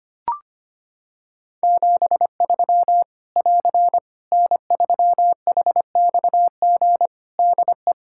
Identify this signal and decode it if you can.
Morse code, heard as a keyed tone, about 25 wpm, 700 Hz. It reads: 73 <AR> N3HXG DE
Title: Practice QSO - Exchange 4 - Ending - 25wpm